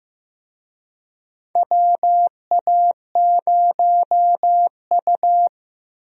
Morse code, heard as a keyed tone, about 15 wpm, 700 Hz. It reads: WA0U
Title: USA Call Signs - 4 Characters - Repeated 3x - 15wpm